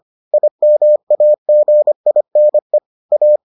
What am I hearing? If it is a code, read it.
Morse code, heard as a keyed tone, about 25 wpm, 600 Hz. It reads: IMAGINE A